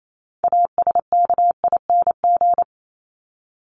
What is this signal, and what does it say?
Morse code, heard as a keyed tone, about 28 wpm, 700 Hz. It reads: ASKING
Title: ING Suffix - Top 100 Words with Base Words 28wpm